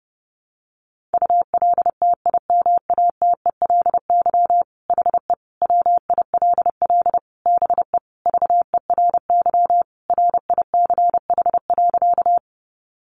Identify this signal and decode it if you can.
Morse code, heard as a keyed tone, about 30 wpm, 700 Hz. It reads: ULTIMATELY HE WILL BE VERY RICH.
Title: Sentences from Top 1200 Words 30wpm